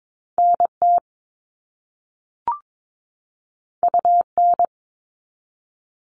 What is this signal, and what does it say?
Morse code, heard as a keyed tone, about 22 wpm, 700 Hz. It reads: NT UN
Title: All Cut Numbers - 2-Digit 22wpm